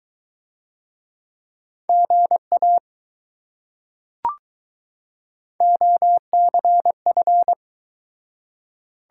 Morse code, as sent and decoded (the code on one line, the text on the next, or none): --. .- / --- -.-. ..-.
GA OCF